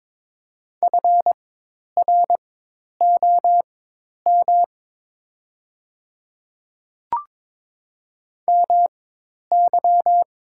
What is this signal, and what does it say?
Morse code, heard as a keyed tone, about 22 wpm, spaced as for 10 wpm, 690 Hz. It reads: FROM MY
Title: 4x Character Spacing - Top 100 Words - 22wpm